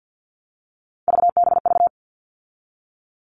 Morse code, heard as a keyed tone, about 50 wpm, 700 Hz. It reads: VBV
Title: All Cut Numbers - 3-Digit 50wpm